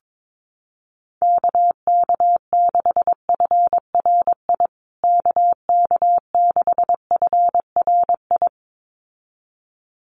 Morse code, heard as a keyed tone, about 22 wpm, 700 Hz. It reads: KK6FRI KK6FRI
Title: USA Call Signs Repeated 2x 22wpm